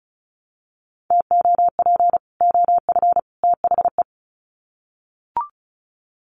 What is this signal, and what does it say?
Morse code, heard as a keyed tone, about 34 wpm, 700 Hz. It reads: TOP OF THE
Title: Sets of 3 Words 35wpm